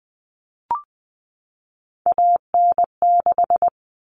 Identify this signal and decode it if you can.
Morse code, heard as a keyed tone, about 20 wpm, 700 Hz. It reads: AN6
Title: All Cut Numbers - 3-Digit 20wpm